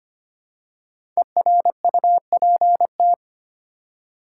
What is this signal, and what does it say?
Morse code, heard as a keyed tone, about 25 wpm, 700 Hz. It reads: ERUPT
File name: 5 Letter Words - First 3 Letters Given - 25wpm